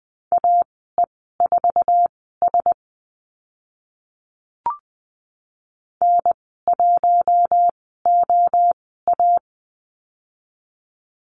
Morse code, heard as a keyed tone, about 20 wpm, spaced as for 14 wpm, 700 Hz. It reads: AE4S N1OA
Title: 069 Lesson 12g - Letter d - Call Signs 20wpm